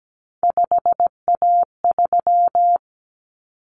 Morse code, heard as a keyed tone, about 17 wpm, 700 Hz. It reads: 5A3
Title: Common Cut Numbers - 3-Digit 17wpm